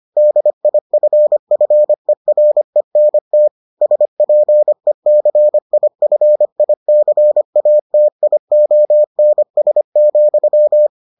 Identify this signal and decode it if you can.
Morse code, heard as a keyed tone, about 25 wpm, 600 Hz. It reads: DIFFERENT SPECIFICATIONS,